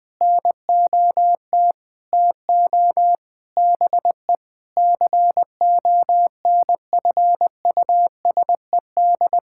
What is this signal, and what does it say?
Morse code, heard as a keyed tone, about 20 wpm, 700 Hz. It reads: NOT TO BE CONFUSED
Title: Sentences with 6-Word Phrases - 20wpm